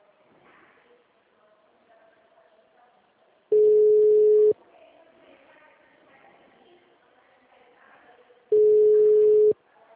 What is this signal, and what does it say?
Ringback tone (Europe, 425 Hz).